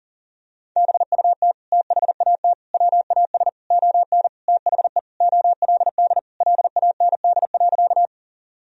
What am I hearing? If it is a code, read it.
Morse code, heard as a keyed tone, about 40 wpm, 700 Hz. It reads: BUT THAT WAS ON THE OLD LAND.
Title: Sentences from Top 1000 Words - Review Part 1 - 40wpm